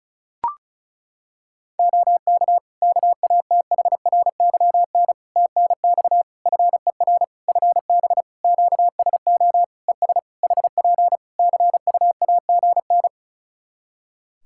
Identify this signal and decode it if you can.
Morse code, heard as a keyed tone, about 34 wpm, 700 Hz. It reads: OK KATHRYN TNX FER FB QSO ES HP CUAGN
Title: K1USN SST QSO Practice 35wpm